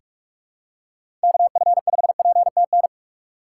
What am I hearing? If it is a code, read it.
Morse code, heard as a keyed tone, about 44 wpm, 700 Hz. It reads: KF5PTN
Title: USA Call Signs - 6x Character Spacing - 45wpm